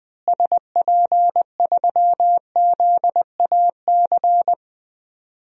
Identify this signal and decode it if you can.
Morse code, heard as a keyed tone, about 21 wpm, 700 Hz. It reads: SP3ZAC
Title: DXCC-100 Call Signs - 6 Characters - Repeated 3x - 20wpm